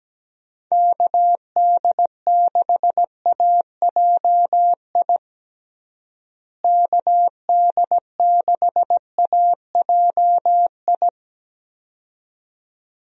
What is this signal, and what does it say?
Morse code, heard as a keyed tone, about 17 wpm, 700 Hz. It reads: KD6AJI KD6AJI
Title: USA Call Signs - 6 Characters - Repeated 2x - 17wpm